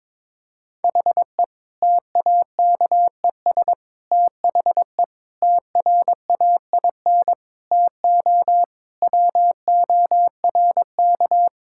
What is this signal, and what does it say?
Morse code, heard as a keyed tone, about 22 wpm, 700 Hz. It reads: HE TAKES THE TRAIN TO WORK
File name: Difficult Sentences with Hints - 22wpm